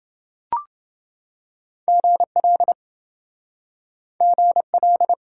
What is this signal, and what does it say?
Morse code, heard as a keyed tone, about 26 wpm, 700 Hz. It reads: GL GL